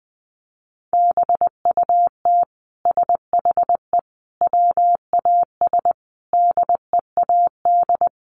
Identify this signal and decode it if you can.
Morse code, heard as a keyed tone, about 19 wpm, 700 Hz. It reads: BUT SHE WAS DEAD
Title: Sentences from Top 1500 Words 20wpm